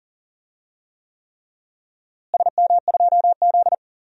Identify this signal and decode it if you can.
Morse code, heard as a keyed tone, about 39 wpm, 700 Hz. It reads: SM2Z
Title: DXCC-100 Call Signs - 4 Characters - Repeated 3x - 40wpm